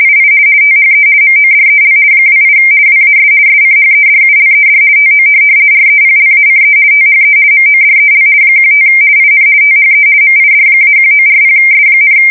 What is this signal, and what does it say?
RTTY, about 50 baud. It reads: II THE MODEL 28 WAS SCHEDULED TO REPLACE THE MODEL 15 IN THE MID-1940S,
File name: fsk170-50B / teletype-corp